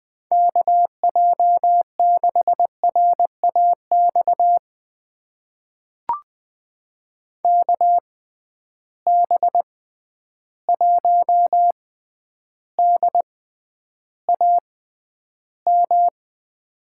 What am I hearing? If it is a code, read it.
Morse code, heard as a keyed tone, about 20 wpm, spaced as for 10 wpm, 700 Hz. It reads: KJ6RAX KB1DAM
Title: USA Call Signs - 6 Characters - 6x Character Spacing - 20wpm